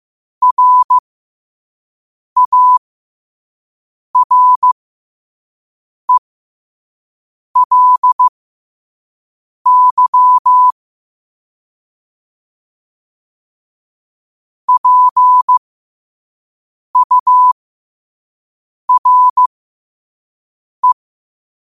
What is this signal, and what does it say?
Morse code, heard as a keyed tone, about 15 wpm, spaced as for 5 wpm, 1000 Hz. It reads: RARELY PURE
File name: qotdfiles / 5wpm